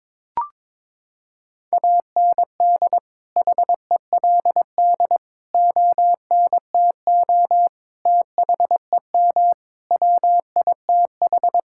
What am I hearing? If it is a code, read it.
Morse code, heard as a keyed tone, about 21 wpm, 700 Hz. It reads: AND HELD ONTO THEM WITH
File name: Sets of 9 Words - 22wpm